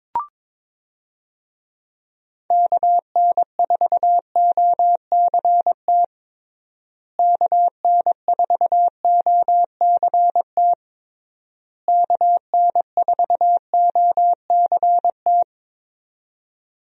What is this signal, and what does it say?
Morse code, heard as a keyed tone, about 22 wpm, 700 Hz. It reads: KN4OCT KN4OCT KN4OCT
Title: USA Call Signs - 6 Characters - Repeated 3x - 22wpm